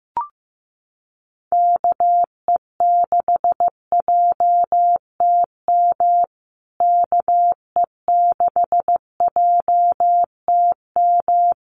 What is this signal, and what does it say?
Morse code, heard as a keyed tone, about 15 wpm, 700 Hz. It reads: KE6JTM KE6JTM
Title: USA Call Signs Repeated 3x 15wpm